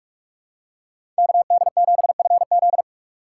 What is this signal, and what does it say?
Morse code, heard as a keyed tone, about 46 wpm, 700 Hz. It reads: KD7FZ
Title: USA Call Signs - 5 Characters - 6x Character Spacing - 45wpm